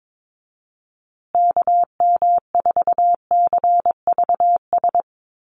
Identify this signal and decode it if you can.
Morse code, heard as a keyed tone, about 22 wpm, 700 Hz. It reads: KM4CVS
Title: USA Call Signs Repeated 3x 22wpm